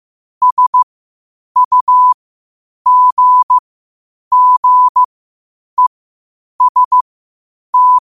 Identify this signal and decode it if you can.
Morse code, heard as a keyed tone, about 15 wpm, spaced as for 8 wpm, 1000 Hz. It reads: SUGGEST
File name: qotdfiles / 8wpm